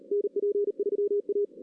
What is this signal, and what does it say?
Morse code, heard as a keyed tone, about 34 wpm, 410 Hz. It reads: NP3A
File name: 2016 arrl dx cw / 2016-ARRL-DX-CW-K5TR- - 0-3